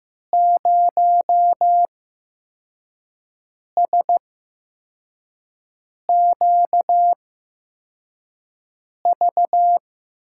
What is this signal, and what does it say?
Morse code, heard as a keyed tone, about 15 wpm, spaced as for 4 wpm, 700 Hz. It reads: 0SQV
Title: USA Call Signs - 5 Characters - 8x Character Spacing - 15wpm